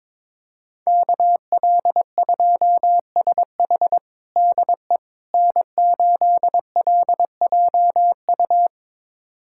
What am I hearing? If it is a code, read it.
Morse code, heard as a keyed tone, about 22 wpm, 700 Hz. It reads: KL2SH DE N8LJU
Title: USA Callsign DE Callsign 22wpm